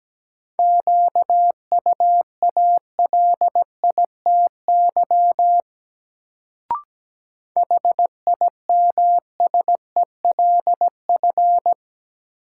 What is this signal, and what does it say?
Morse code, heard as a keyed tone, about 17 wpm, 700 Hz. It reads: QUALITY HIMSELF
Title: Sentences from Top 1500 Words - New Word Drill Alt - 17wpm